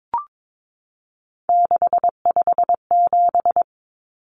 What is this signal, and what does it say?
Morse code, heard as a keyed tone, about 22 wpm, 700 Hz. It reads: 657
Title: Common Cut Numbers - 3-Digit 22wpm